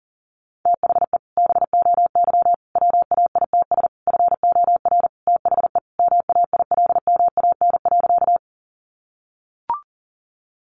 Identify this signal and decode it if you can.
Morse code, heard as a keyed tone, about 40 wpm, 700 Hz. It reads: THE BOY WAITS FOR THE MAILMAN.
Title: Easy Phrases 40wpm